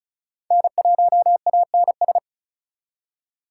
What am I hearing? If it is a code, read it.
Morse code, heard as a keyed tone, about 44 wpm, 700 Hz. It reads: N1ANS